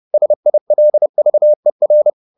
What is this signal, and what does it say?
Morse code, heard as a keyed tone, about 30 wpm, 600 Hz. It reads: SILVER